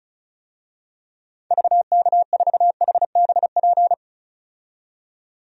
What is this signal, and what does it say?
Morse code, heard as a keyed tone, about 34 wpm, 700 Hz. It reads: VK4HBP
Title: DXCC-100 Call Signs - 6 Characters - 35wpm